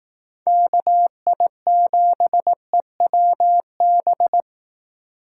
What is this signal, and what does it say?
Morse code, heard as a keyed tone, about 18 wpm, 700 Hz. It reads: KI7EWB
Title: Speed Racing - USA 2x3 Call Signs 15wpm